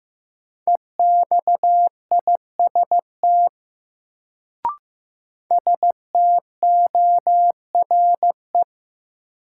Morse code, heard as a keyed tone, about 15 wpm, 700 Hz. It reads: EXIST STORE
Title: Sentences from Top 1000 Words - Review Part 5 - New Word Drill - 15wpm